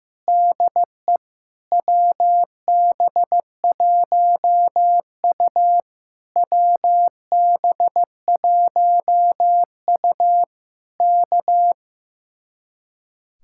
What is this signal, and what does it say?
Morse code, heard as a keyed tone, about 15 wpm, 700 Hz. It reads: DE WB1U WB1U K
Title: Speed-Racing Common QSO Phrases Encore - 15wpm